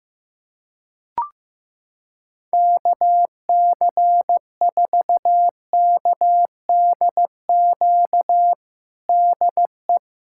Morse code, heard as a keyed tone, about 15 wpm, 700 Hz. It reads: KC4KDQ DE
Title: USA Callsign DE Callsign - 6 Characters - 15wpm